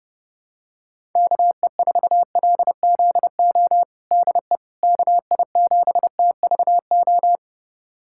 Morse code, heard as a keyed tone, about 30 wpm, 700 Hz. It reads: KE4LZO DE KI7TVO
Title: USA Callsign DE Callsign - 6 Characters - 30wpm